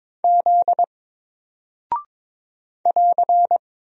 Morse code, heard as a keyed tone, about 22 wpm, 700 Hz. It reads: Z <AR>
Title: Single Character (Letter:Number:Punctuation:Prosign) 22wpm